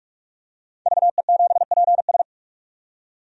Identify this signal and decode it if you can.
Morse code, heard as a keyed tone, about 46 wpm, 700 Hz. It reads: VE7PS
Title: DXCC-100 Call Signs - 5 Characters - 45wpm